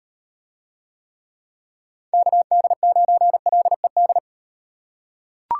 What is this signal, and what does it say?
Morse code, heard as a keyed tone, about 36 wpm, 700 Hz. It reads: KD9LED